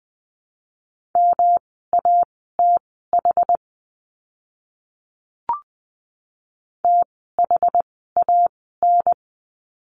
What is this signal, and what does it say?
Morse code, heard as a keyed tone, about 20 wpm, spaced as for 14 wpm, 700 Hz. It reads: MATH THAN
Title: 108 Lesson 18f - Letter m - Words 20wpm